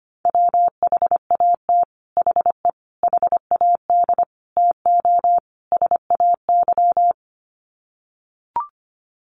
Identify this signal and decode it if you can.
Morse code, heard as a keyed tone, about 25 wpm, 700 Hz. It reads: WHAT HE HAD TO SAY
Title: Sets of 5 Words Encore2 25wpm